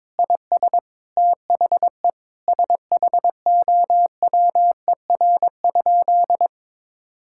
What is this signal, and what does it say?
Morse code, heard as a keyed tone, about 22 wpm, 700 Hz. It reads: IS THE SHOWER?
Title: Sentences from Top 1600 Words 22wpm